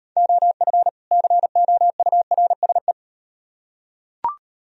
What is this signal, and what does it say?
Morse code, heard as a keyed tone, about 38 wpm, 700 Hz. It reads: OF COURSE